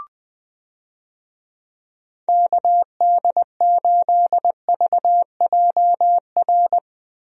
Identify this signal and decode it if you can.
Morse code, heard as a keyed tone, about 21 wpm, 700 Hz. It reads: KD8VJR
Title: USA Call Signs - 6 Characters - Repeated 2x - 20wpm